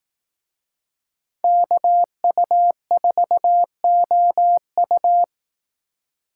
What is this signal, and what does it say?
Morse code, heard as a keyed tone, about 18 wpm, 700 Hz. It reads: KU4OU
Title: Speed Racing - USA 2x2 Call Signs 15wpm